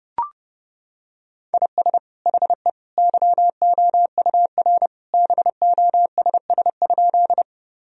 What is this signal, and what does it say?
Morse code, heard as a keyed tone, about 30 wpm, 700 Hz. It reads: IS HE YOUR BOSS?
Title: Sentences from Top 1000 Words - Review Part 4 - 30wpm